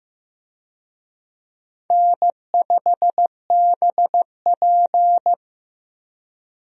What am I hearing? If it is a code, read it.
Morse code, heard as a keyed tone, about 16 wpm, 700 Hz. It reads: N5BP
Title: USA Call Signs - 4 Characters - Repeated 3x - 15wpm